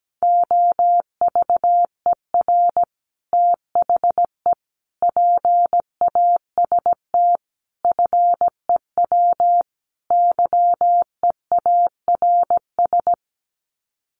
Morse code, heard as a keyed tone, about 17 wpm, 700 Hz. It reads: OVER THE PAST FEW YEARS
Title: Sets of 5 Words Encore 17wpm